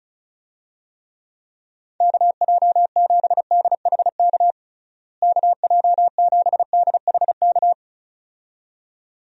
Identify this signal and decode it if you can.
Morse code, heard as a keyed tone, about 35 wpm, 700 Hz. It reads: KJ7DHK KJ7DHK
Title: USA Call Signs - 6 Characters - Repeated 2x - 35wpm